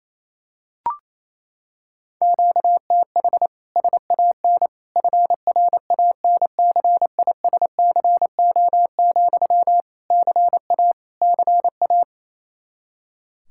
Morse code, heard as a keyed tone, about 27 wpm, 700 Hz. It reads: QTH SAN FRANCISCO, CA CA
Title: K1USN SST QSO Practice 28wpm